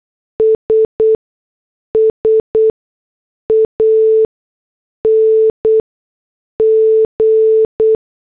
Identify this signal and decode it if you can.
Morse code, heard as a keyed tone, about 8 wpm, 440 Hz. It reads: S S A N G
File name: morse / headlines